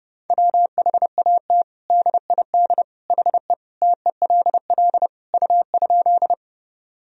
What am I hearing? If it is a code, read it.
Morse code, heard as a keyed tone, about 30 wpm, 700 Hz. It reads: WHAT DID HE TELL U?
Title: Sentences from Top 200 Words and Abbreviations 30wpm